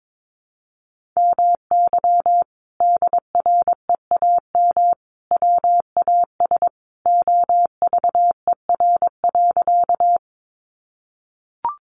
Morse code, heard as a keyed tone, about 22 wpm, 700 Hz. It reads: MY DREAM WAS OVER.
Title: Sentences from Top 1500 Words 22wpm